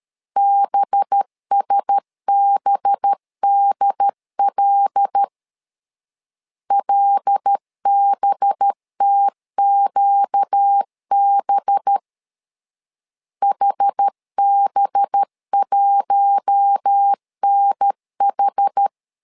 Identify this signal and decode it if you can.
Morse code, heard as a keyed tone, about 13 wpm, 800 Hz. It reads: BSBDL LBTQB HB1NH
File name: LEZIONI / 5 LBVC / 5 LB 12wpm 1906